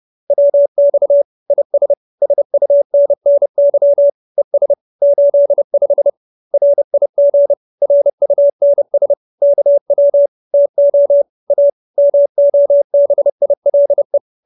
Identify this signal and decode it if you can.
Morse code, heard as a keyed tone, about 30 wpm, 570 Hz. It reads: WX IS SUNNY ES 85 RIG RUNS KW TO A MOBILE
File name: QSO317 30